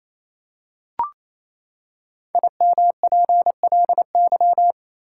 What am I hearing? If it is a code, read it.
Morse code, heard as a keyed tone, about 28 wpm, 700 Hz. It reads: IMPLY
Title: Five Letter Words - 28wpm